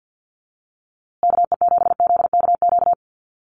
Morse code, heard as a keyed tone, about 50 wpm, 700 Hz. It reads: KE7ZKQ
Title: USA Call Signs Repeated 2x 50wpm